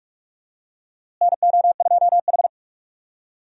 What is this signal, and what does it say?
Morse code, heard as a keyed tone, about 46 wpm, 700 Hz. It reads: NO2H
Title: USA Call Signs - 4 Characters - Repeated 2x - 45wpm